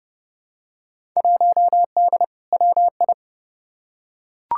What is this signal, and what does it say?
Morse code, heard as a keyed tone, about 30 wpm, 700 Hz. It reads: 1D WI
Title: Field Day Exchanges - 30wpm